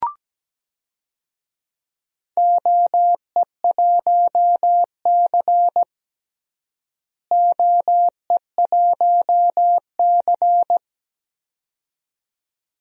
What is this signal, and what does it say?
Morse code, heard as a keyed tone, about 17 wpm, 700 Hz. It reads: OE1C OE1C
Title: DXCC-100 Call Signs - 4 Characters - Repeated 2x - 17wpm